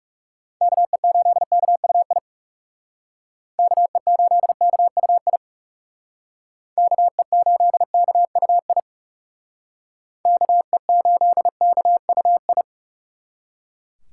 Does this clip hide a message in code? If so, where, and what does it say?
Yes: Morse code from 0.6 to 12.6 s, heard as a keyed tone, about 38 wpm, 700 Hz. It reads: KE8KUI KE8KUI KE8KUI KE8KUI